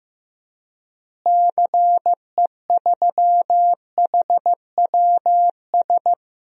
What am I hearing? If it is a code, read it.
Morse code, heard as a keyed tone, about 15 wpm, 700 Hz. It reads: CE3HWS
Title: DXCC-100 Call Signs - 6 Characters - Repeated 3x - 15wpm